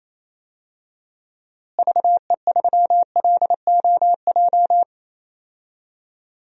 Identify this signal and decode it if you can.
Morse code, heard as a keyed tone, about 28 wpm, 700 Hz. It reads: VE3LOJ
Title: DXCC-100 Call Signs - 6 Characters - Repeated 3x - 28wpm